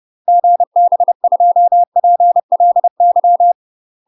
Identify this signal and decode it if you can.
Morse code, heard as a keyed tone, about 30 wpm, 700 Hz. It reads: GB2PLY
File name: UK MASTER QSOs TRAINING / 30WPM / UK MASTER QSOs 0021